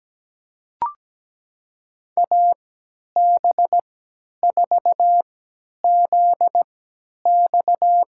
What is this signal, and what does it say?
Morse code, heard as a keyed tone, about 17 wpm, spaced as for 10 wpm, 700 Hz. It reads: AB4ZX
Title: USA Call Signs - 5 Characters - 3x Character Spacing - 17wpm